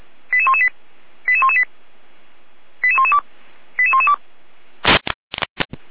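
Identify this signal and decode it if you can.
Five-tone selective call (ZVEI), heard as quick runs of five tones, tones of about 70 ms. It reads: 88118 88118 88112 88112
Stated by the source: ZVEI five-tone call 88491 received by SDR.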